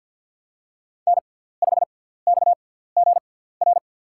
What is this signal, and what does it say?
Morse code, heard as a keyed tone, about 47 wpm, spaced as for 17 wpm, 690 Hz. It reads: N5XGR
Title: USA Call Signs - 6x Character Spacing - 50wpm